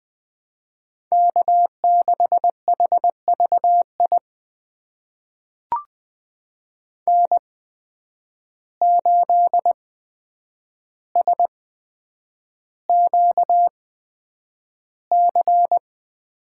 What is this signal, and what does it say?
Morse code, heard as a keyed tone, about 20 wpm, spaced as for 9 wpm, 700 Hz. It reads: K6HVI N8SQC
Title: USA Call Signs - 5 Characters - 8x Character Spacing - 20wpm